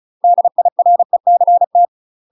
Morse code, heard as a keyed tone, about 36 wpm, 700 Hz. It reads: DIRECT
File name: UK MASTER QSOs TRAINING / 35WPM / UK MASTER QSOs 0007